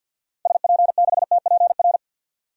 Morse code, heard as a keyed tone, about 51 wpm, 700 Hz. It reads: SP6TPR
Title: DXCC-100 Call Signs - 6 Characters - 50wpm